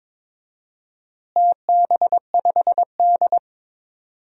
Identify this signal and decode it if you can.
Morse code, heard as a keyed tone, about 21 wpm, 700 Hz. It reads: TB5D
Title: DXCC-100 Call Signs - 4 Characters - Repeated 3x - 22wpm